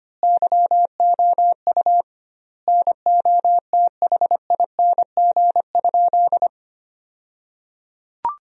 Morse code, heard as a keyed tone, about 25 wpm, 700 Hz. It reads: YOU NOTHING?
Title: 2x Word-Spacing - Sentences from Top 500 Words - 25wpm